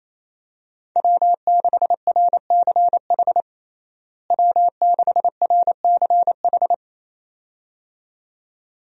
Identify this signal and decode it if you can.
Morse code, heard as a keyed tone, about 28 wpm, 700 Hz. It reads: W6RCH W6RCH
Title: USA Call Signs - 5 Characters - Repeated 2x - 28wpm